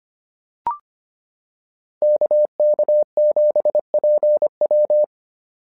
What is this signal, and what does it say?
Morse code, heard as a keyed tone, about 25 wpm, 600 Hz. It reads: KK7PW
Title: SOTA QSO Practice 25wpm